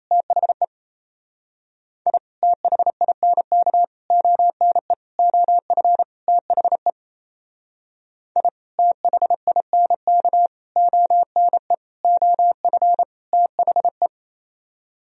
Morse code, heard as a keyed tone, about 32 wpm, 700 Hz. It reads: THE I THINK ONE OF THE I THINK ONE OF THE
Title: Speed-Racing - Sets of 5 Words 28wpm